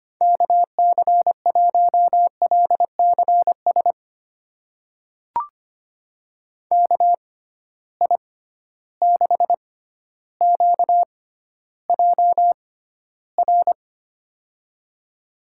Morse code, heard as a keyed tone, about 25 wpm, spaced as for 13 wpm, 700 Hz. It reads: KC1LCS KI6QJR
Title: USA Call Signs - 6 Characters - 6x Character Spacing - 25wpm